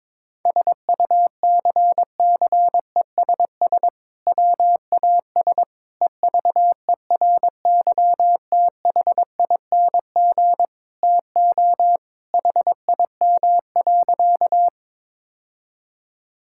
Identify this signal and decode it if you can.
Morse code, heard as a keyed tone, about 22 wpm, 700 Hz. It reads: SUCCESS WAS EVERYTHING TO HIM.
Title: Sentences from Top 1000 Words - Review Part 6 - 22wpm